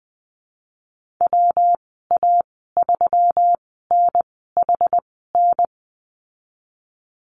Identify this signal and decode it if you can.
Morse code, heard as a keyed tone, about 20 wpm, spaced as for 15 wpm, 700 Hz. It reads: WA3NHN
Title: 122 Lesson 20g - Number 3 - Call Signs 20wpm